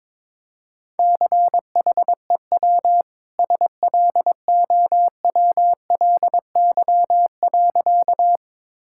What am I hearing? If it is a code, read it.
Morse code, heard as a keyed tone, about 22 wpm, 700 Hz. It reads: CHEW SLOWLY.
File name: practice / 22wpm / Easy Sentences with Hints - 22wpm